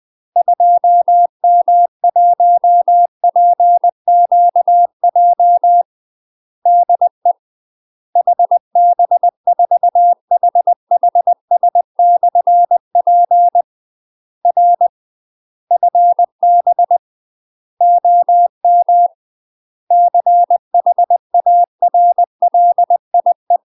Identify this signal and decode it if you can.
Morse code, heard as a keyed tone, about 20 wpm, 700 Hz. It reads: 2M1PQJ DE HB4HHS/P R FB OM CHARLIE
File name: UK MASTER QSOs TRAINING / 20WPM / UK MASTER QSOs 0018